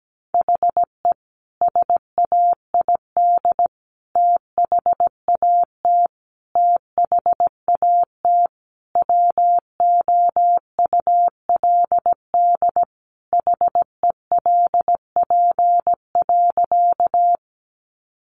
Morse code, heard as a keyed tone, about 17 wpm, 700 Hz. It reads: HE SAID THAT THAT WOULD HELP.
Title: Sentences from Top 200 Words 17wpm